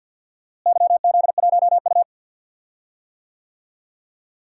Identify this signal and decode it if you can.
Morse code, heard as a keyed tone, about 48 wpm, 700 Hz. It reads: YZ1U
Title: DXCC-100 Call Signs - 4 Characters - Repeated 2x - 50wpm